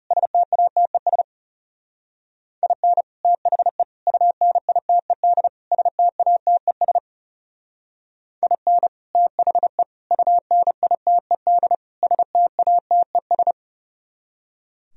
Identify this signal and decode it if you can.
Morse code, heard as a keyed tone, about 40 wpm, 700 Hz. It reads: STATES IN THE UNITED STATES IN THE UNITED STATES